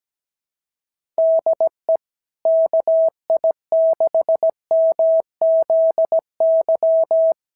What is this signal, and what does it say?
Morse code, heard as a keyed tone, about 15 wpm, 650 Hz. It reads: DE KI6MZY
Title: SOTA QSO Practice 17wpm